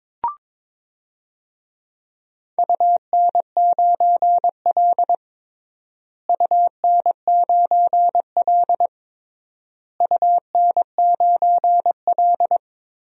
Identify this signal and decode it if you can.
Morse code, heard as a keyed tone, about 22 wpm, 700 Hz. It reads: UN9L UN9L UN9L
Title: DXCC-100 Call Signs - 4 Characters - Repeated 3x - 22wpm